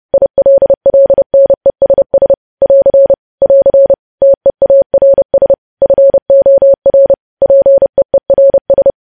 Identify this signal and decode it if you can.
Morse code, heard as a keyed tone, about 30 wpm, 550 Hz. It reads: ILLNESS <AR> <AR> TEARS FOR PEERS
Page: news headlines in morse code at 30 wpm